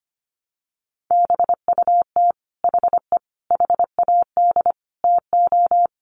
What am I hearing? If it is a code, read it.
Morse code, heard as a keyed tone, about 25 wpm, 700 Hz. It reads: BUT HE HAD TO
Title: Sentences from Top 100 Words 25wpm